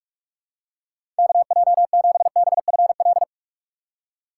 Morse code, heard as a keyed tone, about 46 wpm, 700 Hz. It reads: KJ7BFL
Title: USA Call Signs - 4x Character Spacing - 45wpm